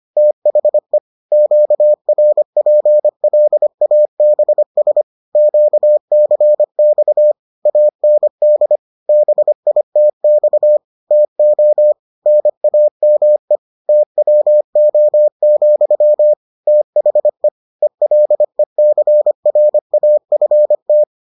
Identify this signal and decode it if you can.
Morse code, heard as a keyed tone, about 25 wpm, 600 Hz. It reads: THE QRPLABS QCX AND BITX TO NAME TWO, THE ELECRAFT